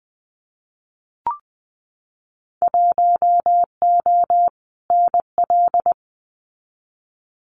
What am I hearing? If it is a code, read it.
Morse code, heard as a keyed tone, about 20 wpm, 700 Hz. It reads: 1O NL
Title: Winter Field Day Exchanges Encore 20wpm